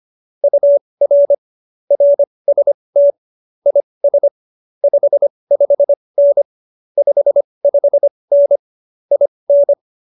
Morse code, heard as a keyed tone, about 25 wpm, spaced as for 20 wpm, 570 Hz. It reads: UR RST IS 55N 55N IN
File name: qso220b 20